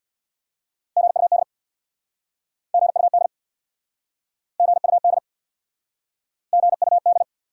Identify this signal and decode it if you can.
Morse code, heard as a keyed tone, about 69 wpm, 700 Hz. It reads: GUD GUD GUD GUD